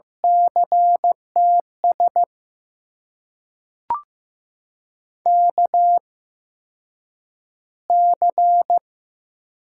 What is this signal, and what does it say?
Morse code, heard as a keyed tone, about 15 wpm, spaced as for 8 wpm, 700 Hz. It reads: CTS KC
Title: USA Call Signs - 6 Characters - 8x Character Spacing - 15wpm